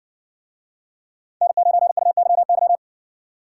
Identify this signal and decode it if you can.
Morse code, heard as a keyed tone, about 55 wpm, 700 Hz. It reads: N9UYY